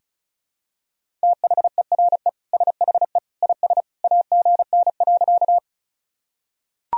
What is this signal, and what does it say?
Morse code, heard as a keyed tone, about 36 wpm, 700 Hz. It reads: THERE SHE IS AGN.
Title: Sentences from Top 200 Words and Abbreviations 35wpm